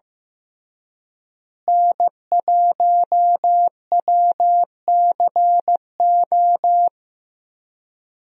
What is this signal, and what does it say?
Morse code, heard as a keyed tone, about 15 wpm, 700 Hz. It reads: N1WCO
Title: USA Call Signs - 5 Characters - Repeated 3x - 15wpm